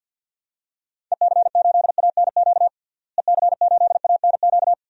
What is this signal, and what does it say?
Morse code, heard as a keyed tone, about 49 wpm, 700 Hz. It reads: EC8ANQ EC8ANQ
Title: DXCC-100 Call Signs - 6 Characters - Repeated 2x - 50wpm